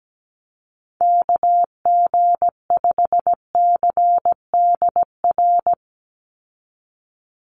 Morse code, heard as a keyed tone, about 17 wpm, 700 Hz. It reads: KG5CDR
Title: USA Call Signs 17wpm